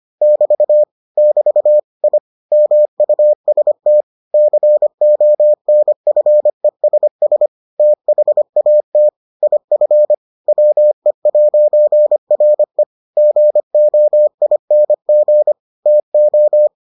Morse code, heard as a keyed tone, about 25 wpm, 600 Hz. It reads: <BT> <BT> I MUST CONFESS THAT IF WE'RE GOING TO